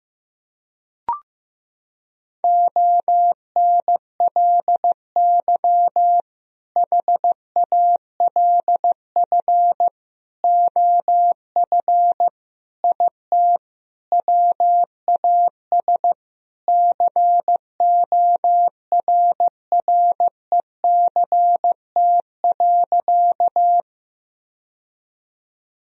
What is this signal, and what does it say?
Morse code, heard as a keyed tone, about 15 wpm, 700 Hz. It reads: ONLY HALF OF IT WAS CORRECT.
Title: Sentences from Top 1000 Words - Review Part 2 - 15wpm